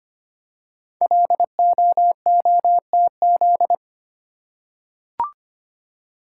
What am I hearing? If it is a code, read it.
Morse code, heard as a keyed tone, about 25 wpm, 700 Hz. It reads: LOOTZ
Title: Code Groups - 5 Letters - 25wpm